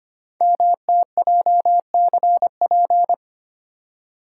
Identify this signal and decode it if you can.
Morse code, heard as a keyed tone, about 25 wpm, 700 Hz. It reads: MTJCP
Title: Code Groups - 5 Letters - 25wpm